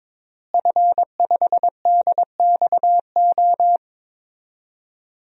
Morse code, heard as a keyed tone, about 22 wpm, 700 Hz. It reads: F5DXO
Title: DXCC-100 Call Signs - 5 Characters - Repeated 3x - 22wpm